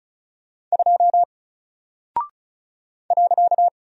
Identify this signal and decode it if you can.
Morse code, heard as a keyed tone, about 36 wpm, 700 Hz. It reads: 2 .
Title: Single Character (Letter:Number:Punctuation:Prosign) Alt 35wpm